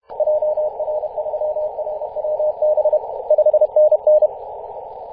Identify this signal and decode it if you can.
Morse code, heard as a keyed tone, about 31 wpm, 610 Hz. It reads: JR3RWB 5NN
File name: 2016/Apr/08 0544z 4W/JI1AVY 21.011MHz CW